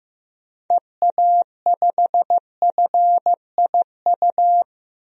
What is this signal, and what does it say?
Morse code, heard as a keyed tone, about 15 wpm, 700 Hz. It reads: EA5FIU
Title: DXCC-100 Call Signs - 6 Characters - Repeated 2x - 15wpm